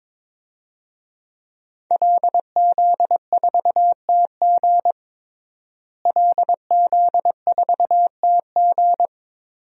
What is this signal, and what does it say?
Morse code, heard as a keyed tone, about 22 wpm, 700 Hz. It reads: LZ4TG LZ4TG
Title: DXCC-100 Call Signs - 5 Characters - Repeated 2x - 22wpm